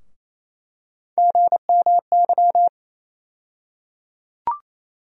Morse code, heard as a keyed tone, about 28 wpm, 700 Hz. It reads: GMY
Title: USA Call Sign Postfixes - 3 Letters 28wpm